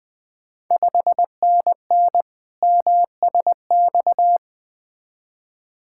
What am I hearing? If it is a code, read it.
Morse code, heard as a keyed tone, about 20 wpm, 700 Hz. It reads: 5NN MSX
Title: Ontario QSO Party Exchange 20wpm